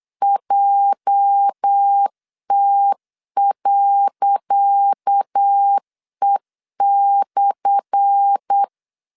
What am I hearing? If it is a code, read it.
Morse code, heard as a keyed tone, about 8 wpm, 800 Hz. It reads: JT.E/
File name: LEZIONI / 10 SIGNS / 10C DOT 8wpm 3649